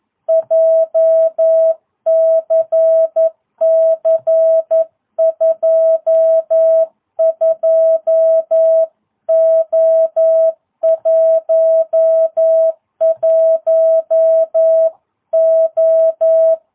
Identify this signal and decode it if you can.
Morse code, heard as a keyed tone, about 11 wpm, 650 Hz. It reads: JCC 22O11O
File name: jh8cbh / 2022cw